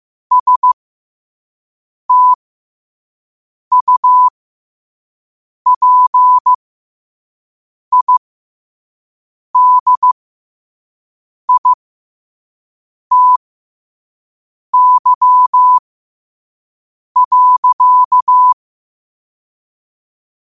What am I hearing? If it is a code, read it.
Morse code, heard as a keyed tone, about 15 wpm, spaced as for 5 wpm, 1000 Hz. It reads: STUPIDITY.